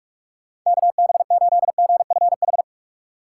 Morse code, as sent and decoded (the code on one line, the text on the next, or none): -.- -... ---.. --. ..-. ....
KB8GFH